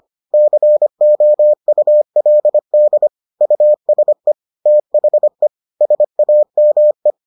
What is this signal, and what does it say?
Morse code, heard as a keyed tone, about 25 wpm, 600 Hz. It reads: COULD USE THE SAME